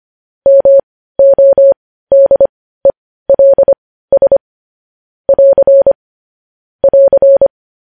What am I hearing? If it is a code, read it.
Morse code, heard as a keyed tone, about 25 wpm, spaced as for 15 wpm, 550 Hz. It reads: MODELS <AR> <AR>
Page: News Headlines in Morse Code at 15 WPM